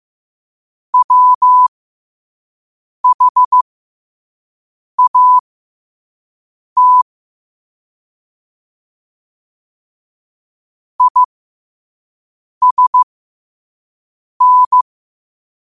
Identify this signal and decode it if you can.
Morse code, heard as a keyed tone, about 15 wpm, spaced as for 5 wpm, 1000 Hz. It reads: WHAT ISN
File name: qotdfiles / 5wpm